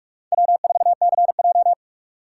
Morse code, heard as a keyed tone, about 46 wpm, 700 Hz. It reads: W4CJ
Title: USA Call Signs - 4 Characters - 4x Character Spacing - 45wpm